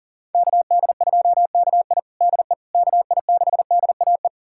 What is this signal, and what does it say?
Morse code, heard as a keyed tone, about 40 wpm, 700 Hz. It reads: KD2KI DE KI6DAE
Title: USA Callsign DE Callsign 40wpm